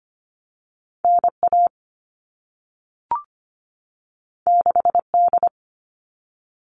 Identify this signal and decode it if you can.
Morse code, heard as a keyed tone, about 25 wpm, 700 Hz. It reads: NA 6D
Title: All Cut Numbers - 2-Digit 25wpm